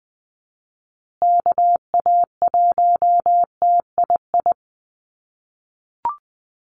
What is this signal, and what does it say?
Morse code, heard as a keyed tone, about 20 wpm, 700 Hz. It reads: KA1TII
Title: USA Call Signs Repeated 3x 20wpm